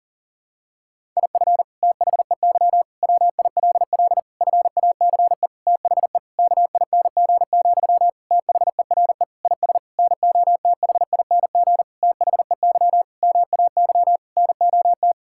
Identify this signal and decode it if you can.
Morse code, heard as a keyed tone, about 40 wpm, 700 Hz. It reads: IF THEY WILL FACE THE KING, THERE IS NOTHING THEY MAY NOT
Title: Sentences from Top 1000 Words - Review Part 6 - 40wpm